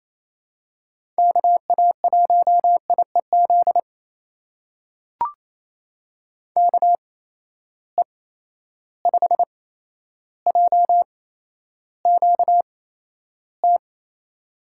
Morse code, heard as a keyed tone, about 28 wpm, spaced as for 12 wpm, 700 Hz. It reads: KA1IEZ KE5JQT
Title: USA Call Signs - 6 Characters - 8x Character Spacing - 28wpm